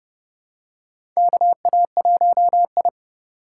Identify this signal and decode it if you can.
Morse code, heard as a keyed tone, about 30 wpm, 700 Hz. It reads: KA1I
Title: USA Call Signs - 2x Character Spacing - 30wpm